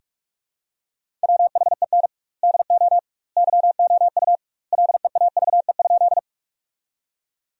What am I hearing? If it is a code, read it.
Morse code, heard as a keyed tone, about 41 wpm, 700 Hz. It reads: WHEN DO YOU LEAVE?
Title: 2x Word-Spacing - Sentences from Top 400 Words - 45wpm